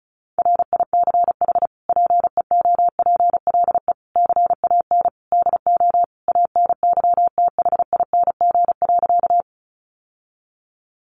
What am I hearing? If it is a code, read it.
Morse code, heard as a keyed tone, about 34 wpm, 700 Hz. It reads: RICH PEOPLE CAN DO ANYTHING.
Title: Sentences from Top 1100 Words 35wpm